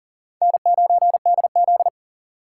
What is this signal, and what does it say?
Morse code, heard as a keyed tone, about 40 wpm, 700 Hz. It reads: N9DZ
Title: USA Call Signs - 4 Characters - 3x Character Spacing - 40wpm